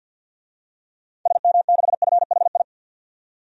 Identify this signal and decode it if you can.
Morse code, heard as a keyed tone, about 46 wpm, 700 Hz. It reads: SM6FHI
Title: DXCC-100 Call Signs - 6 Characters - 50wpm